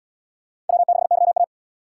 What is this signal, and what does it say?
Morse code, heard as a keyed tone, about 71 wpm, 700 Hz. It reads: W6YA